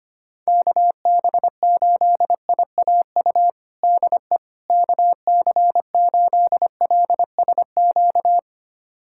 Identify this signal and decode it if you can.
Morse code, heard as a keyed tone, about 25 wpm, 700 Hz. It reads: KB8IAU DE KC8LSQ
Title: USA Callsign DE Callsign - 6 Characters - 25wpm